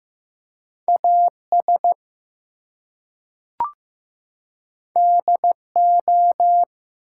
Morse code, heard as a keyed tone, about 15 wpm, 700 Hz. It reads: AS DO
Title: Two Letter Words - 15wpm